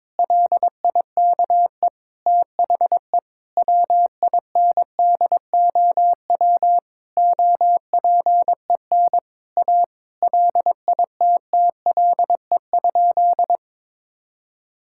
Morse code, heard as a keyed tone, about 22 wpm, 700 Hz. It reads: LIKE THE WINDOW OPEN A LITTLE?
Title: Sentences from Top 1600 Words 22wpm